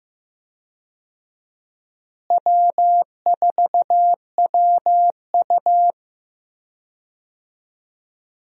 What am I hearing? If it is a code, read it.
Morse code, heard as a keyed tone, about 15 wpm, 700 Hz. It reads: W4WU
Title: USA Call Signs - 4 Characters - Repeated 2x - 15wpm